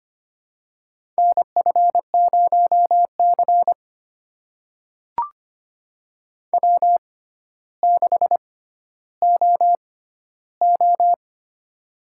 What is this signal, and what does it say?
Morse code, heard as a keyed tone, about 25 wpm, spaced as for 13 wpm, 700 Hz. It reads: NF0C W6OO
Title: USA Call Signs - 4 Characters - 6x Character Spacing - 25wpm